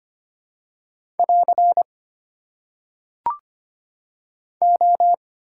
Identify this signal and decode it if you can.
Morse code, heard as a keyed tone, about 25 wpm, 700 Hz. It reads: <AR> O
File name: Top 90 QSO Elements - 4x Character Spacing - 25wpm